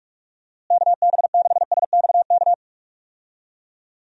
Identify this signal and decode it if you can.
Morse code, heard as a keyed tone, about 46 wpm, 700 Hz. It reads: KB6SXK
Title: USA Call Signs - 6 Characters - Repeated 2x - 45wpm